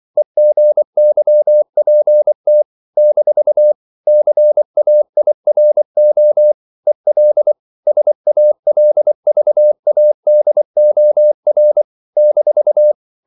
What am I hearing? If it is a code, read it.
Morse code, heard as a keyed tone, about 24 wpm, 600 Hz. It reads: EGYPT - CAIRO EL SALVADOR -
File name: CW / Practice / NationalCapitals24